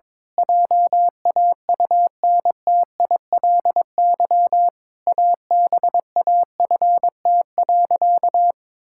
Morse code, heard as a keyed tone, about 22 wpm, 700 Hz. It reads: JAUNTILY ABAFT.
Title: Pangram Challenge v2 - 22wpm